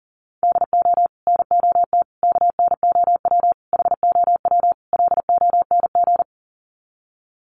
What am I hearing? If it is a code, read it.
Morse code, heard as a keyed tone, about 37 wpm, 700 Hz. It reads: DO NOT KNOW HOW LONG
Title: Sets of 5 Words Encore2 40wpm